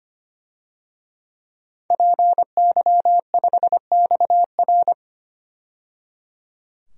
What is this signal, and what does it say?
Morse code, heard as a keyed tone, about 25 wpm, 700 Hz. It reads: PY5XR
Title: DXCC-100 Call Signs - 5 Characters - 25wpm